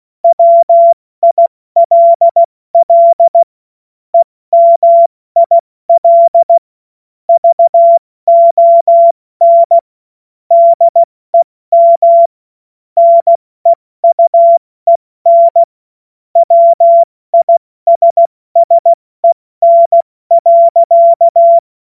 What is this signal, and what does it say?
Morse code, heard as a keyed tone, about 16 wpm, 680 Hz. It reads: WILL EMIL VON DEM NEUEN WISSEN.
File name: Morsekurs / Witze / Witz 01. März 2023 70 F80